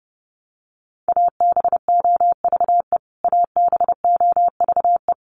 Morse code, heard as a keyed tone, about 30 wpm, 700 Hz. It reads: ABOVE ABOVE
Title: Top 500 Words Repeated 30wpm